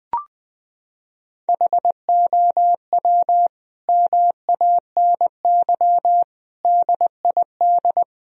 Morse code, heard as a keyed tone, about 20 wpm, 700 Hz. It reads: HOW MANY DID
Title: Sentences from Top 1000 Words - Review Part 1 - 20wpm